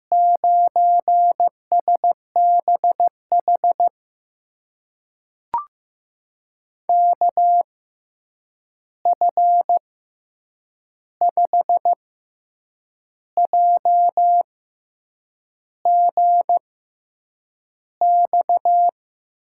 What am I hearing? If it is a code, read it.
Morse code, heard as a keyed tone, about 15 wpm, spaced as for 7 wpm, 700 Hz. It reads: 9SBH KF5JGX
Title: USA Call Signs - 6 Characters - 6x Character Spacing - 15wpm